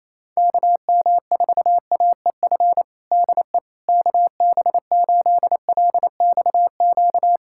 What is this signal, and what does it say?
Morse code, heard as a keyed tone, about 27 wpm, 700 Hz. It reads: KM4AEF DE KB8LXQ
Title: USA Callsign DE Callsign - 6 Characters - 28wpm